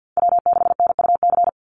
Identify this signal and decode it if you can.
Morse code, heard as a keyed tone, about 49 wpm, 700 Hz. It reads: W6NUC
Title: Call Signs Repeated 3x - 50wpm